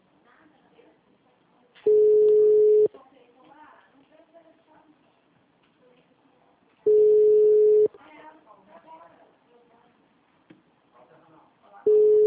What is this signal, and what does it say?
Ringback tone (Europe, 425 Hz), with speech.